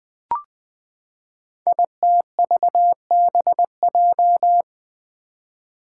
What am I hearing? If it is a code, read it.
Morse code, heard as a keyed tone, about 20 wpm, 700 Hz. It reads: ITVBJ
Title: Code Groups - 5 Letters - 20wpm